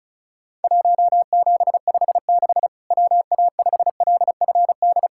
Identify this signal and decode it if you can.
Morse code, heard as a keyed tone, about 36 wpm, 700 Hz. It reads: 1756 WA5LFD
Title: Worked All Europe DX Contest - QTC Exchange - 35wpm